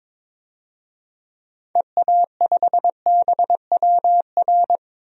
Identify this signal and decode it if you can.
Morse code, heard as a keyed tone, about 21 wpm, 700 Hz. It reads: EA5BWR
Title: DXCC-100 Call Signs - 6 Characters - Repeated 3x - 22wpm